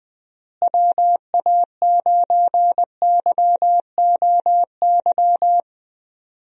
Morse code, heard as a keyed tone, about 20 wpm, 700 Hz. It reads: WA9YOY
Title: USA Call Signs - 6 Characters - Repeated 2x - 20wpm